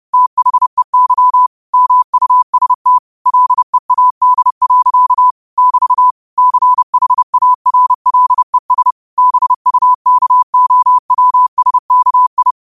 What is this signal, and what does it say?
Morse code, heard as a keyed tone, about 30 wpm, 1000 Hz. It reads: THEY MUST LEAD. <BT> CHARLES BUKOWSKI
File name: qotdfiles / 30wpm